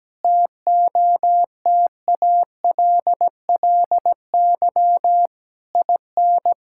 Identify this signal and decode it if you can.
Morse code, heard as a keyed tone, about 17 wpm, 700 Hz. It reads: TOTALLY IN
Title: Sentences from Top 1600 Words 17wpm